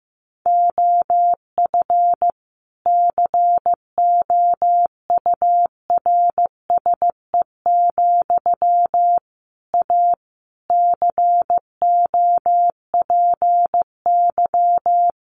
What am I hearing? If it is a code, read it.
Morse code, heard as a keyed tone, about 15 wpm, 700 Hz. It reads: OF COURSE, A COPY
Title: Sentences from Top 1400 Words 15wpm